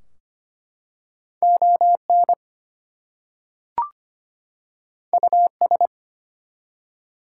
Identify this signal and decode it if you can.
Morse code, heard as a keyed tone, about 25 wpm, 700 Hz. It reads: ON US
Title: Two Letter Words - 25wpm